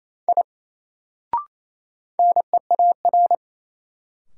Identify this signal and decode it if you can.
Morse code, heard as a keyed tone, about 28 wpm, 700 Hz. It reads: I NEAR
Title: Sentences from Top 200 Words - New Word Drill - 28wpm